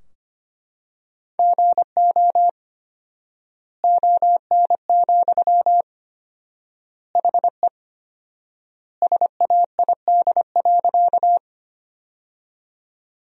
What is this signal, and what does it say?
Morse code, heard as a keyed tone, about 25 wpm, 700 Hz. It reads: GO ON, HE SAID.
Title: 4x Word-Spacing - Sentences from Top 100 Words - 25wpm